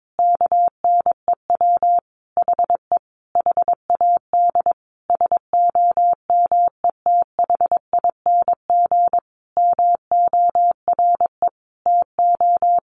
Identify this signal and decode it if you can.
Morse code, heard as a keyed tone, about 22 wpm, 700 Hz. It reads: KNEW HE HAD SOMETHING MORE TO
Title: Sentences from Top 600 Words 22wpm